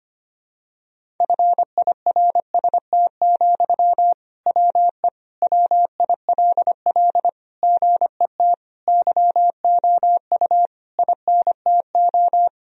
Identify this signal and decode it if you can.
Morse code, heard as a keyed tone, about 25 wpm, 700 Hz. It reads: FIRST, WE WILL GET YOU INTO
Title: Sentences from Top 1000 Words - Review Part 1 - 25wpm